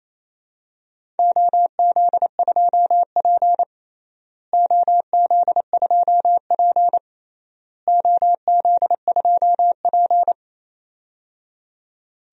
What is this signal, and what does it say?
Morse code, heard as a keyed tone, about 28 wpm, 700 Hz. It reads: OZ2P OZ2P OZ2P
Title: DXCC-100 Call Signs - 4 Characters - Repeated 3x - 28wpm